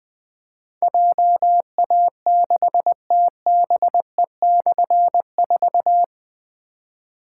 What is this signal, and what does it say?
Morse code, heard as a keyed tone, about 20 wpm, 700 Hz. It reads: JA6TBE/4
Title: Difficult Call Signs 20wpm